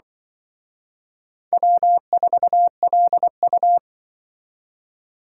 Morse code, heard as a keyed tone, about 24 wpm, 700 Hz. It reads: W4LU
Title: Speed Racing - USA 1x2 Call Signs 20wpm